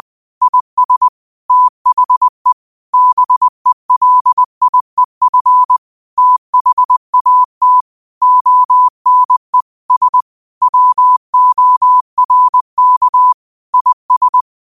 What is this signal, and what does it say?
Morse code, heard as a keyed tone, about 20 wpm, 1000 Hz. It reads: IS THE BELIEF THAT ONE S WORK IS
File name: qotdfiles / 20wpm